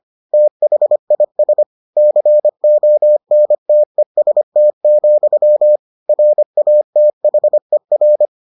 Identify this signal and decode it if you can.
Morse code, heard as a keyed tone, about 25 wpm, 600 Hz. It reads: THIS CONTEST, RATHER